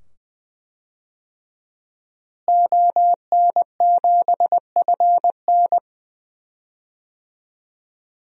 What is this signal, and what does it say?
Morse code, heard as a keyed tone, about 20 wpm, 700 Hz. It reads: ON7FN
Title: DXCC-100 Call Signs - 5 Characters - Repeated 2x - 20wpm